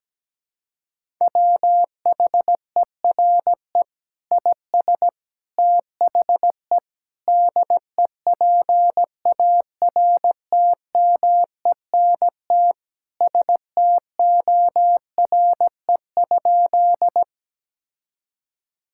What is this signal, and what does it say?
Morse code, heard as a keyed tone, about 17 wpm, 700 Hz. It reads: WHERE IS THE DEPARTMENT STORE?
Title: Sentences from Top 1000 Words - Review Part 5 - 17wpm